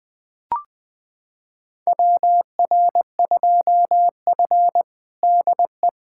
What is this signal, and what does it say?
Morse code, heard as a keyed tone, about 20 wpm, 700 Hz. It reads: WR2F DE
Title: USA Callsign DE Callsign - 4 Characters - 20wpm